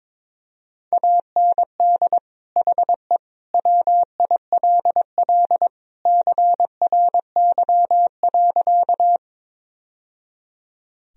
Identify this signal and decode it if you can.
Morse code, heard as a keyed tone, about 22 wpm, 700 Hz. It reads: AND HE WILL CRY.
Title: Sentences from Top 1000 Words - Review Part 2 - 22wpm